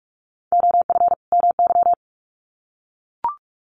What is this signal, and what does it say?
Morse code, heard as a keyed tone, about 44 wpm, 700 Hz. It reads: OF MY
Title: Sets of 2 Words 45wpm